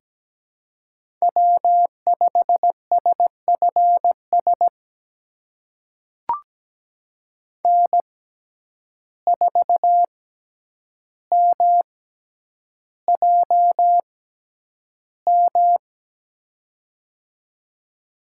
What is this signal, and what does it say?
Morse code, heard as a keyed tone, about 17 wpm, spaced as for 9 wpm, 700 Hz. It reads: W5SFS N4MJM
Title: USA Call Signs - 6x Character Spacing - 17wpm